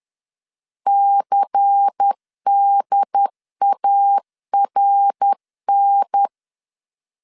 Morse code, heard as a keyed tone, about 10 wpm, 800 Hz. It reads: CDARN
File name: LEZIONI / 6 PKX0 / 6 P 10wpm 2432